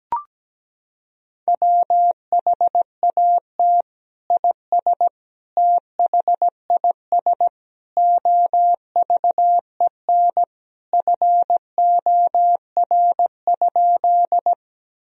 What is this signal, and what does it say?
Morse code, heard as a keyed tone, about 17 wpm, 700 Hz. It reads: WHAT IS THIS OVEN FOR?
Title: Sentences from Top 1000 Words - Review Part 4 - 17wpm